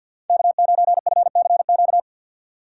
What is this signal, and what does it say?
Morse code, heard as a keyed tone, about 53 wpm, 700 Hz. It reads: K9FCQ
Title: USA Call Signs - 5 Characters - 4x Character Spacing - 50wpm